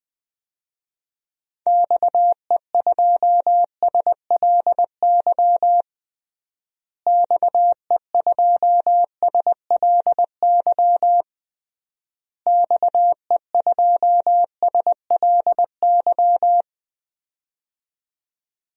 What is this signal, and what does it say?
Morse code, heard as a keyed tone, about 20 wpm, 700 Hz. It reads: XE2SLY XE2SLY XE2SLY
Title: DXCC-100 Call Signs - 6 Characters - Repeated 3x - 20wpm